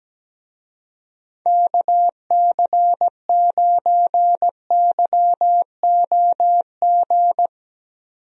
Morse code, heard as a keyed tone, about 17 wpm, 700 Hz. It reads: KC9YOG
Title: USA Call Signs - 6 Characters - 17wpm